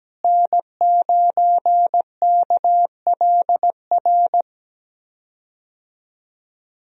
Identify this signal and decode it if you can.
Morse code, heard as a keyed tone, about 17 wpm, 700 Hz. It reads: N9KLR
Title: USA Call Signs - 5 Characters - Repeated 3x - 17wpm